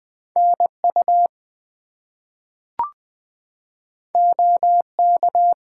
Morse code, heard as a keyed tone, about 20 wpm, 700 Hz. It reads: NU OK
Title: US States and Canadian Provinces 20wpm